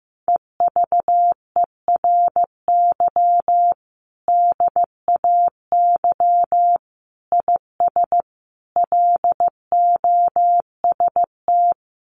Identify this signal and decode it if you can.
Morse code, heard as a keyed tone, about 15 wpm, 700 Hz. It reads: EVERY DAY IS LOST
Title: Sentences from Top 1400 Words 15wpm